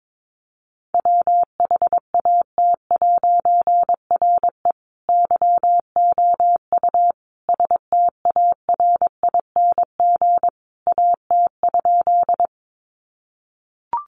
Morse code, heard as a keyed tone, about 22 wpm, 700 Hz. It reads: WHAT'RE YOU STARING AT?
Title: A Sound Of Thunder 22wpm